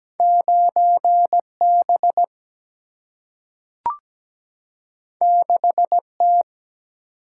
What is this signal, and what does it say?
Morse code, heard as a keyed tone, about 17 wpm, 700 Hz. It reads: 9B 6T
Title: Single-Letter Single-Number 17wpm